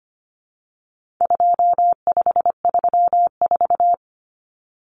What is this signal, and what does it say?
Morse code, heard as a keyed tone, about 25 wpm, 700 Hz. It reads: 2534
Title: Numbers - 4-Digit 25wpm